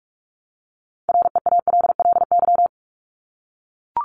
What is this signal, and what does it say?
Morse code, heard as a keyed tone, about 44 wpm, 700 Hz. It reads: REALLY
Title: Top 501-750 Words 45wpm